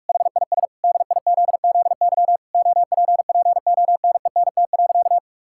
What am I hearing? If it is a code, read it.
Morse code, heard as a keyed tone, about 48 wpm, 700 Hz. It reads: HIS DIZZY OPPONENT.
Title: Pangram Challenge v2 - 45wpm